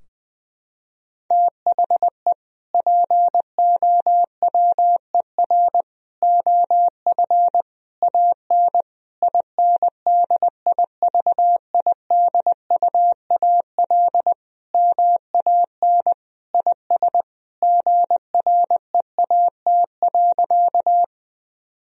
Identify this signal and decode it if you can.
Morse code, heard as a keyed tone, about 20 wpm, 700 Hz. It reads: THE POWER OF AN INDIVIDUAL MAN IS GREAT.
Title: Sentences from Top 1000 Words - Review Part 4 - 20wpm